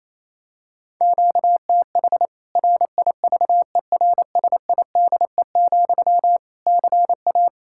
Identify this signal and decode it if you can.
Morse code, heard as a keyed tone, about 27 wpm, 700 Hz. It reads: QTH RIVERSIDE, CA
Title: K1USN SST QSO Practice 28wpm